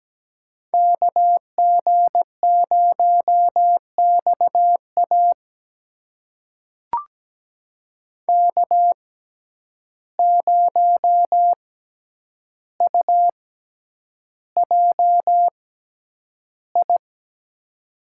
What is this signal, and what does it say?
Morse code, heard as a keyed tone, about 17 wpm, spaced as for 9 wpm, 700 Hz. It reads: KG0XA K0UJI
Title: USA Call Signs - 5 Characters - 6x Character Spacing - 17wpm